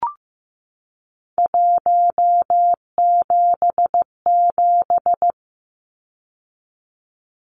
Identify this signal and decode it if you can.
Morse code, heard as a keyed tone, about 14 wpm, 700 Hz. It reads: 177
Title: Numbers - 3-Digit 15wpm